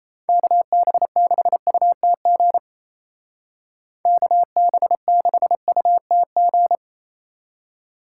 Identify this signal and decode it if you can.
Morse code, heard as a keyed tone, about 36 wpm, 700 Hz. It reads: KB6UTG KB6UTG